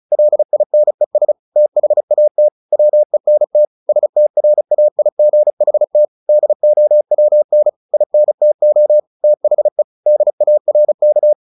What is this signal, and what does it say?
Morse code, heard as a keyed tone, about 34 wpm, 600 Hz. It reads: LINES THAT WENT STRAIGHT DOWN INTO THE DARK
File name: the old man and the sea / the old man and the sea 1 35wpm